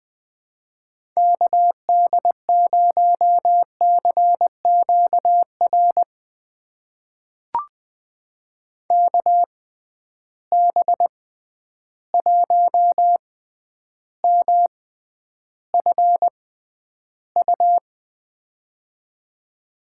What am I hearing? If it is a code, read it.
Morse code, heard as a keyed tone, about 20 wpm, spaced as for 10 wpm, 700 Hz. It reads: KD0CQR KB1MFU
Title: USA Call Signs - 6 Characters - 6x Character Spacing - 20wpm